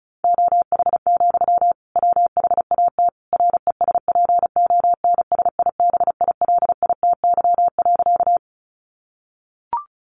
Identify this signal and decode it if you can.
Morse code, heard as a keyed tone, about 34 wpm, 700 Hz. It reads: OH, WHAT RESPONSIBILITY.
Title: Sentences from Top 1200 Words 35wpm